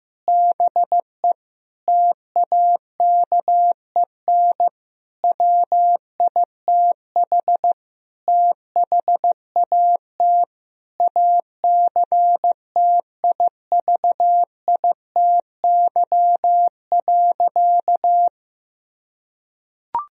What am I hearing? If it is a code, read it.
Morse code, heard as a keyed tone, about 15 wpm, 700 Hz. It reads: BE TAKEN WITH THAT ACTIVITY.
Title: Sentences from Top 1000 Words - Review Part 4 - 15wpm